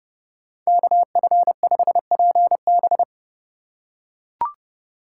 Morse code, heard as a keyed tone, about 27 wpm, 700 Hz. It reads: KF5PB